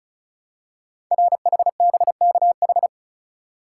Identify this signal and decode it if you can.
Morse code, heard as a keyed tone, about 36 wpm, 700 Hz. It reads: RHBKH
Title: Code Groups - 5 Letters - 35wpm